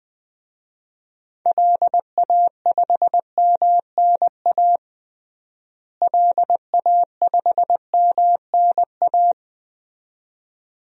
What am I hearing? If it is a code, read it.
Morse code, heard as a keyed tone, about 20 wpm, 700 Hz. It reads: LA5MNA LA5MNA
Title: DXCC-100 Call Signs - 6 Characters - Repeated 2x - 20wpm